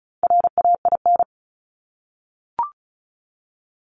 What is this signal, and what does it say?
Morse code, heard as a keyed tone, about 36 wpm, 700 Hz. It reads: RAIN
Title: Multi-letter Phonemes 35wpm